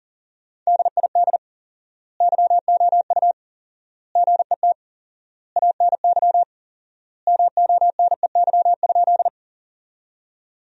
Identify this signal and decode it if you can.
Morse code, heard as a keyed tone, about 40 wpm, 700 Hz. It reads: DID YOU GET ANY MONEY?
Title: 4x Word-Spacing - Sentences from Top 400 Words - 40wpm